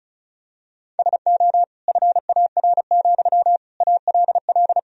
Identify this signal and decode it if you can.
Morse code, heard as a keyed tone, about 34 wpm, 700 Hz. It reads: SO FAR, ALL
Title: Sentences from Top 1000 Words - Review Part 4 - 35wpm